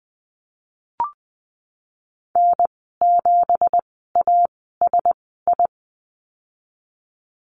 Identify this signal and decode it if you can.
Morse code, heard as a keyed tone, about 20 wpm, spaced as for 15 wpm, 700 Hz. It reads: N7ASI
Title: 173 Lesson 28g - Number 9 - Call Signs 20wpm